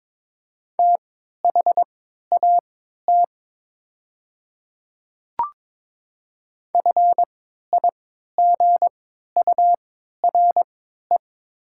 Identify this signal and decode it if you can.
Morse code, heard as a keyed tone, about 22 wpm, spaced as for 13 wpm, 700 Hz. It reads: THAT FIGURE
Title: 3x Character Spacing - Top 500 Words - 22wpm